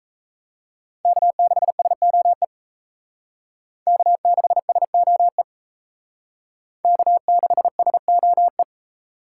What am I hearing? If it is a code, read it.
Morse code, heard as a keyed tone, about 35 wpm, 700 Hz. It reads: K6SOE K6SOE K6SOE
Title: Speed Racing - USA 1x3 Call Signs 28wpm